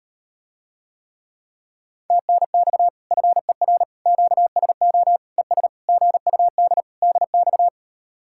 Morse code, heard as a keyed tone, about 39 wpm, 700 Hz. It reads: TNX FER QSO ES GUD DX